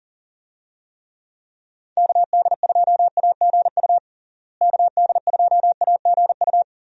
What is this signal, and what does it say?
Morse code, heard as a keyed tone, about 40 wpm, 700 Hz. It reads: KD2AGU KD2AGU
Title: USA Call Signs - 6 Characters - Repeated 2x - 40wpm